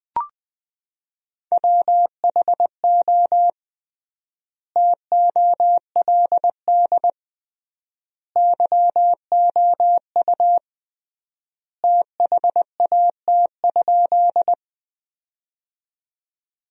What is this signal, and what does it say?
Morse code, heard as a keyed tone, about 21 wpm, 700 Hz. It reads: WHO TOLD YOU THAT?
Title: 3x Word-Spacing - Sentences from Top 400 Words - 20wpm